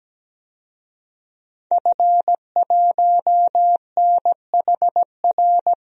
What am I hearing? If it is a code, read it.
Morse code, heard as a keyed tone, about 17 wpm, 700 Hz. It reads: F1NHR
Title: DXCC-100 Call Signs - 5 Characters - Repeated 2x - 17wpm